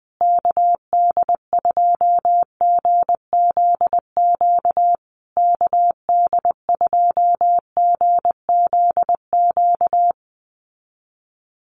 Morse code, heard as a keyed tone, about 20 wpm, 700 Hz. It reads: KD2GZQ KD2GZQ
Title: USA Call Signs Repeated 3x 20wpm